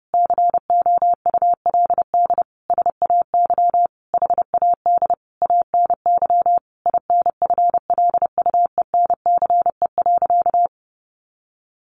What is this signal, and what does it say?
Morse code, heard as a keyed tone, about 30 wpm, 700 Hz. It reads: COULD SAY HAD ANY INFLUENCE.
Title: Sentences from Top 1100 Words 30wpm